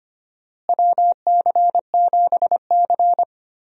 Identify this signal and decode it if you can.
Morse code, heard as a keyed tone, about 25 wpm, 700 Hz. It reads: WC7C
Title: USA Call Signs - 4 Characters - Repeated 3x - 25wpm